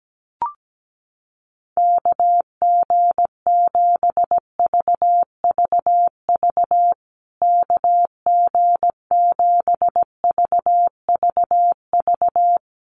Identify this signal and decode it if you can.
Morse code, heard as a keyed tone, about 17 wpm, 700 Hz. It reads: KG7VVV KG7VVV
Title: USA Call Signs Repeated 3x 17wpm